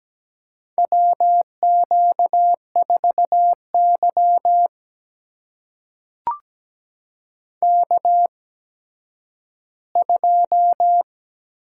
Morse code, heard as a keyed tone, about 17 wpm, spaced as for 10 wpm, 700 Hz. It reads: WQ4Y K2
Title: USA Call Signs - 4 Characters - 8x Character Spacing - 17wpm